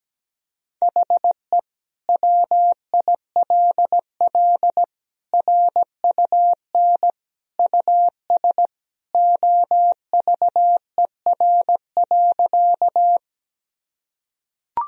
Morse code, heard as a keyed tone, about 17 wpm, 700 Hz. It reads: HE WILL RUN US OVER.
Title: Sentences from Top 1000 Words - Review Part 1 - 17wpm